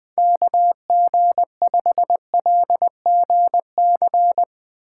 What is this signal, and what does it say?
Morse code, heard as a keyed tone, about 20 wpm, 700 Hz. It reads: KG5LGC